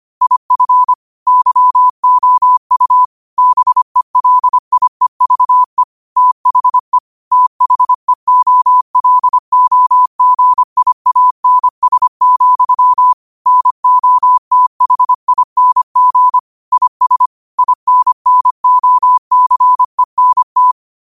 Morse code, heard as a keyed tone, about 25 wpm, 1000 Hz. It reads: IF YOU BELIEVE THE THEOLOGIANS, NOTHING IS INNOCENT
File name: qotdfiles / 25wpm